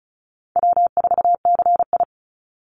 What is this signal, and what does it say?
Morse code, heard as a keyed tone, about 34 wpm, 700 Hz. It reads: W4CI
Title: Call Signs Repeated 2x - 35wpm